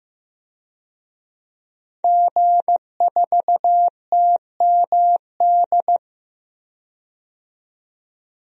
Morse code, heard as a keyed tone, about 15 wpm, 700 Hz. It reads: G4TMD
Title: DXCC-100 Call Signs - 5 Characters - 15wpm